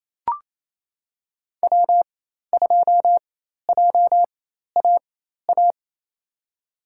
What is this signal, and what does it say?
Morse code, heard as a keyed tone, about 27 wpm, spaced as for 13 wpm, 700 Hz. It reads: W2JAA
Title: USA Call Signs - 5 Characters - 4x Character Spacing - 28wpm